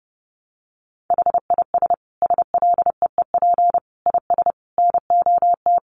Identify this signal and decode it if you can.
Morse code, heard as a keyed tone, about 30 wpm, 700 Hz. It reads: HIS SLEEP IS NOT
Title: Sentences from Top 700 Words 30wpm